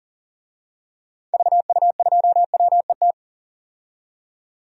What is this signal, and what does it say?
Morse code, heard as a keyed tone, about 40 wpm, 700 Hz. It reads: VU2WET
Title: DXCC-100 Call Signs - 6 Characters - 40wpm